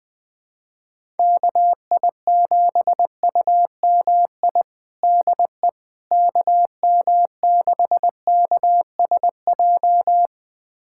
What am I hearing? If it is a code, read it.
Morse code, heard as a keyed tone, about 20 wpm, 700 Hz. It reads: KI7UMI DE KM6KSJ
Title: USA Callsign DE Callsign - 6 Characters - 20wpm